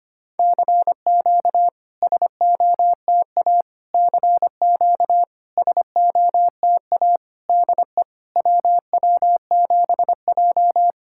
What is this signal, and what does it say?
Morse code, heard as a keyed tone, about 25 wpm, 700 Hz. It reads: CQ SOTA CQ SOTA DE WW7J
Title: SOTA QSO Practice 25wpm